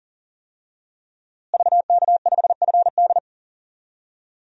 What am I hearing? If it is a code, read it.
Morse code, heard as a keyed tone, about 40 wpm, 700 Hz. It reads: VK5FD
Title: DXCC-100 Call Signs - 5 Characters - Repeated 3x - 40wpm